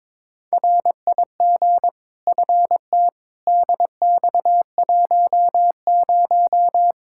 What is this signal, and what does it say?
Morse code, heard as a keyed tone, about 22 wpm, 700 Hz. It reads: RIG FT DX10
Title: K1USN SST QSO Practice 22wpm